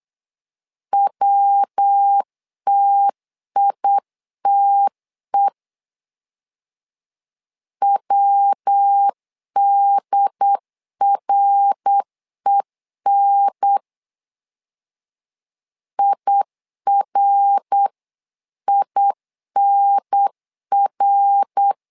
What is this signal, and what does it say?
Morse code, heard as a keyed tone, about 8 wpm, 800 Hz. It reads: WTITE WDREN IRINR
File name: LEZIONI / 3 MWRO / 3 MWR 8wpm 2444